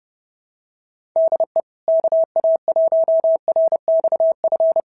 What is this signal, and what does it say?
Morse code, heard as a keyed tone, about 30 wpm, 650 Hz. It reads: DE KA1RXF
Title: SOTA QSO Practice 30wpm